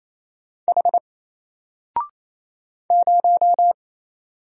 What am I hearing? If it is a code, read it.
Morse code, heard as a keyed tone, about 28 wpm, 700 Hz. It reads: H 0
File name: Single-Letter-Number-28wpm